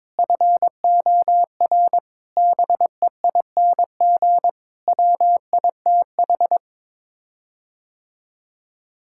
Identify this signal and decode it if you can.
Morse code, heard as a keyed tone, about 22 wpm, 700 Hz. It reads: FOR BEING WITH
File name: Sets of 9 Words - 22wpm